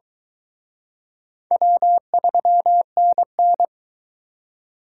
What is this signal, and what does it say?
Morse code, heard as a keyed tone, about 23 wpm, 700 Hz. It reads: W3NN